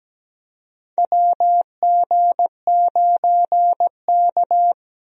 Morse code, heard as a keyed tone, about 17 wpm, 700 Hz. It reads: WG9K
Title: USA Call Signs - 4 Characters - 17wpm